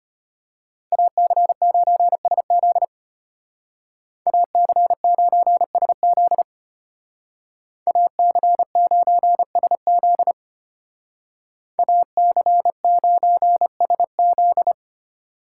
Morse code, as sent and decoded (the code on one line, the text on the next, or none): .- -.-. ----. ... --.. / .- -.-. ----. ... --.. / .- -.-. ----. ... --.. / .- -.-. ----. ... --..
AC9SZ AC9SZ AC9SZ AC9SZ